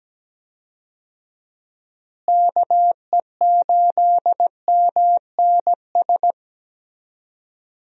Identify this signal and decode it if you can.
Morse code, heard as a keyed tone, about 17 wpm, 700 Hz. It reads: KE8MNS
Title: USA Call Signs - 6 Characters - Repeated 3x - 17wpm